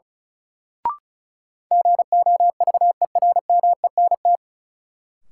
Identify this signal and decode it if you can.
Morse code, heard as a keyed tone, about 36 wpm, 700 Hz. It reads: GOVERMENT
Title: Sentences from Top 1300 Words - New Word Drill Alt - 35wpm